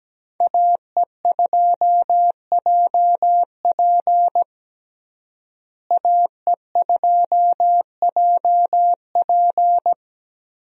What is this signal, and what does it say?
Morse code, heard as a keyed tone, about 17 wpm, 700 Hz. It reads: AE2JP AE2JP
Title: USA Call Signs - 5 Characters - Repeated 3x - 17wpm